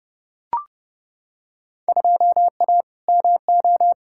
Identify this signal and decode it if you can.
Morse code, heard as a keyed tone, about 30 wpm, 700 Hz. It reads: 2A MO
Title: Field Day Exchanges Encore - 30wpm